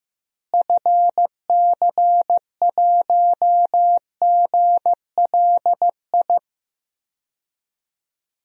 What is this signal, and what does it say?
Morse code, heard as a keyed tone, about 15 wpm, 700 Hz. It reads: FC1GLI
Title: DXCC-100 Call Signs - 6 Characters - Repeated 2x - 15wpm